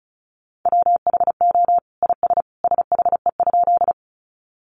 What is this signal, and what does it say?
Morse code, heard as a keyed tone, about 34 wpm, 700 Hz. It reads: WHO IS SHE?
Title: Sentences from Top 100 Words 35wpm